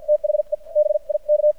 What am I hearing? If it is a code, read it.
Morse code, heard as a keyed tone, about 55 wpm, spaced as for 42 wpm, 600 Hz. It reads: THE BIK
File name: archived / CRSnet